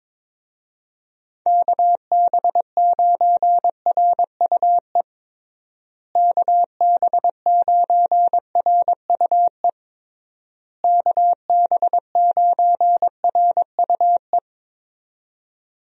Morse code, heard as a keyed tone, about 22 wpm, 700 Hz. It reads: KB9RUE KB9RUE KB9RUE
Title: USA Call Signs - 6 Characters - Repeated 3x - 22wpm